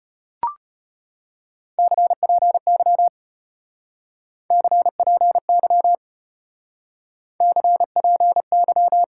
Morse code, heard as a keyed tone, about 33 wpm, 700 Hz. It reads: CPY CPY CPY